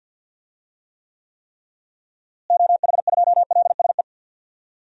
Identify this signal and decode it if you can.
Morse code, heard as a keyed tone, about 51 wpm, 700 Hz. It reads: OH2LSE
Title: DXCC-100 Call Signs - 6 Characters - Repeated 3x - 50wpm